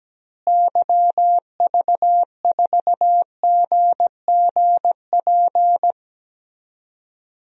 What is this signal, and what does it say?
Morse code, heard as a keyed tone, about 17 wpm, 700 Hz. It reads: YV4GGP
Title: DXCC-100 Call Signs - 6 Characters - Repeated 2x - 17wpm